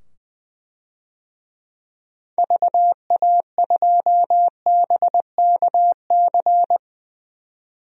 Morse code, heard as a keyed tone, about 21 wpm, 700 Hz. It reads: VA2BKC
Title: DXCC-100 Call Signs - 6 Characters - 20wpm